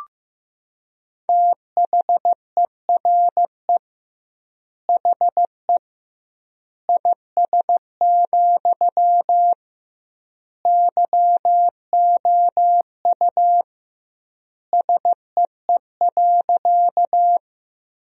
Morse code, heard as a keyed tone, about 15 wpm, 700 Hz. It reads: THERE HE IS, YOU SEE.
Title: 2x Word-Spacing - Sentences from Top 100 Words - 15wpm